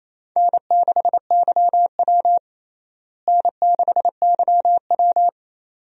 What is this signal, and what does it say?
Morse code, heard as a keyed tone, about 28 wpm, 700 Hz. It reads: N6YW N6YW
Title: USA Call Signs - 4 Characters - Repeated 2x - 28wpm